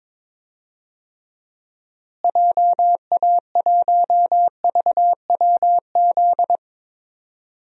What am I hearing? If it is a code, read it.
Morse code, heard as a keyed tone, about 22 wpm, 700 Hz. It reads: JA1VWZ
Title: DXCC-100 Call Signs - 6 Characters - Repeated 3x - 22wpm